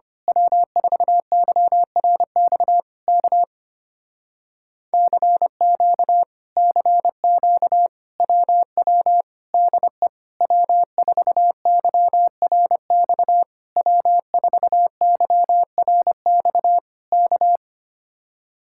Morse code, heard as a keyed tone, about 24 wpm, 700 Hz. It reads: W4YRX K CQ CQ WW DE W4YRX W4YRX K
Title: Speed-Racing Common QSO Phrases - 25wpm